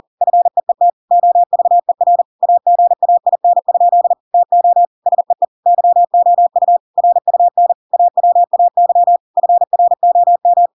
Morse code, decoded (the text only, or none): FEET OVER AGAIN? TO SEE YOU RUN AWAY FROM